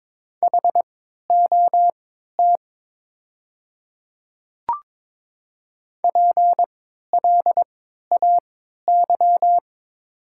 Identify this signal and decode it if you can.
Morse code, heard as a keyed tone, about 22 wpm, spaced as for 13 wpm, 690 Hz. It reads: HOT PLAY
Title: 3x Character Spacing - Top 200 Words - 22wpm